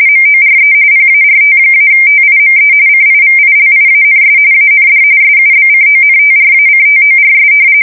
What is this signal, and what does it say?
RTTY, about 45.45 baud. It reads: MORE PART-TIME CIVILIANS AND SUBCONTRACTORS
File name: fsk170 / battleshipnj